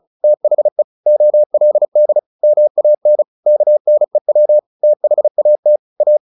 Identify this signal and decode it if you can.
Morse code, heard as a keyed tone, about 34 wpm, 600 Hz. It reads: THE OLD MAN KNEW THAT A
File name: the old man and the sea 2 35wpm